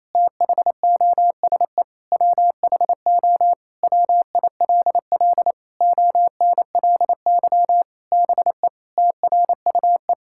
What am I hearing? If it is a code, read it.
Morse code, heard as a keyed tone, about 28 wpm, 700 Hz. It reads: THOSE WHO WILL ONLY BE TRUE
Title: Sentences from Top 1000 Words - Review Part 6 - 28wpm